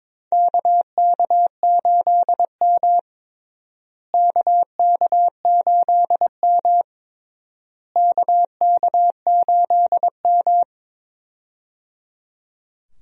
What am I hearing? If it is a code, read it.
Morse code, heard as a keyed tone, about 22 wpm, 700 Hz. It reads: KK8M KK8M KK8M
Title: USA Call Signs - 4 Characters - Repeated 3x - 22wpm